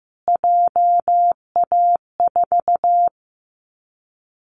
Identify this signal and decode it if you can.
Morse code, heard as a keyed tone, about 15 wpm, 700 Hz. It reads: JA4
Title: Callsign Prefixes for Top 25 DXCC - Repeated 3x - 15wpm